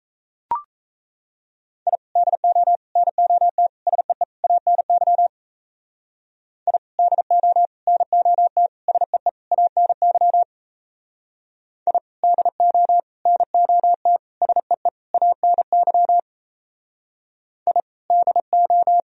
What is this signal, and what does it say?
Morse code, heard as a keyed tone, about 33 wpm, 700 Hz. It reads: I DO NOT SEE ANY I DO NOT SEE ANY I DO NOT SEE ANY I DO
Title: Speed-Racing - Sets of 5 Words 28wpm